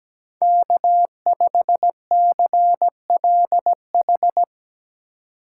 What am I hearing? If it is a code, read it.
Morse code, heard as a keyed tone, about 19 wpm, 700 Hz. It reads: K5CLH
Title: Speed Racing - USA 1x3 Call Signs 17wpm